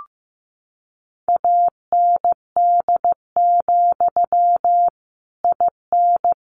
Morse code, heard as a keyed tone, about 15 wpm, 700 Hz. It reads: AND, IN
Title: Sentences from Top 1200 Words 15wpm